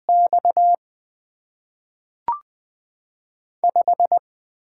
Morse code, heard as a keyed tone, about 20 wpm, 700 Hz. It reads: X 5
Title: Top 20 QSO Elements - 6x Character Spacing - 20wpm